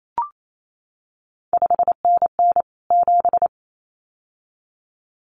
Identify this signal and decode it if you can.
Morse code, heard as a keyed tone, about 28 wpm, 700 Hz. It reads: 5NN 7
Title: CQ WW DX Contest Exchange Encore - 28wpm